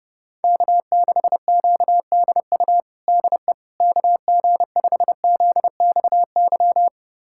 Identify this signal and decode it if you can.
Morse code, heard as a keyed tone, about 30 wpm, 700 Hz. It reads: K6QDU DE KG5ZXY
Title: USA Callsign DE Callsign 30wpm